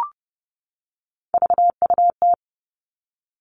Morse code, heard as a keyed tone, about 30 wpm, 700 Hz. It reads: VUT
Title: Three Challenging Letter-Number Combinations 30wpm